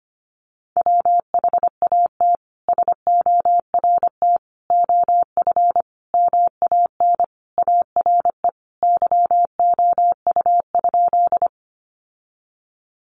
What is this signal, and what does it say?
Morse code, heard as a keyed tone, about 25 wpm, 700 Hz. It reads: WHAT SORT OF MAN ARE YOU?
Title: Sentences from Top 1300 Words 25wpm